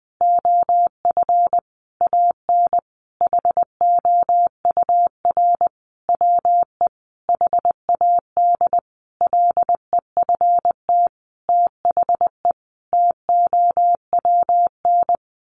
Morse code, heard as a keyed tone, about 20 wpm, 700 Hz. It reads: OF AN HOUR WE HAD LEFT THE TOWN
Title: Sentences from Top 1300 Words 20wpm